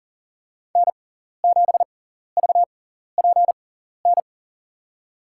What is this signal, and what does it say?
Morse code, heard as a keyed tone, about 37 wpm, spaced as for 13 wpm, 700 Hz. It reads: N7VPN
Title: USA Call Signs - 5 Characters - 6x Character Spacing - 40wpm